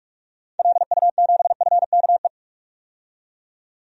Morse code, heard as a keyed tone, about 41 wpm, 700 Hz. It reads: LU7FKE
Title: DXCC-100 Call Signs - 6 Characters - 45wpm